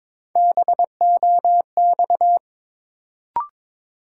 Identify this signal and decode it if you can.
Morse code, heard as a keyed tone, about 22 wpm, 700 Hz. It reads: BOX
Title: Sentences from Top 500 Words - New Word Drill - 22wpm